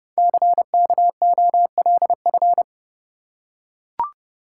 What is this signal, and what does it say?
Morse code, heard as a keyed tone, about 28 wpm, 700 Hz. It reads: CKOLF
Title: Code Groups - 5 Letters - 30wpm